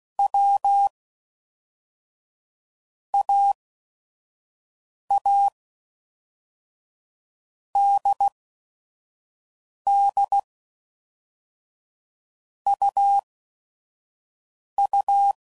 Morse code, heard as a keyed tone, about 16 wpm, 800 Hz. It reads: W A A D D U U
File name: Basic Letter Groups By Blocks / Groups@5 / AWUNDG@5 f15